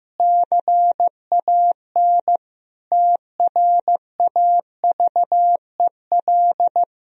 Morse code, heard as a keyed tone, about 15 wpm, 700 Hz. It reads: CAN TRAVEL
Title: Sentences from Top 1000 Words - Review Part 2 - 15wpm